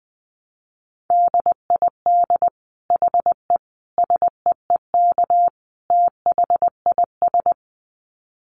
Morse code, heard as a keyed tone, about 20 wpm, 700 Hz. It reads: DID HE SEEK THIS
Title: Dit Challenge 20wpm